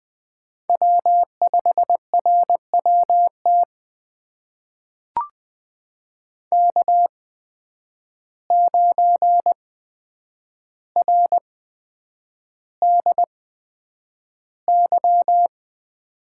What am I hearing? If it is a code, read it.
Morse code, heard as a keyed tone, about 20 wpm, spaced as for 9 wpm, 700 Hz. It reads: W5RWT K9RDY
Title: USA Call Signs - 5 Characters - 8x Character Spacing - 20wpm